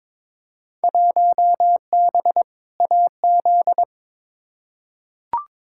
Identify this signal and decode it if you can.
Morse code, heard as a keyed tone, about 22 wpm, 700 Hz. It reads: 1B AZ
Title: Field Day Exchanges Encore - 22wpm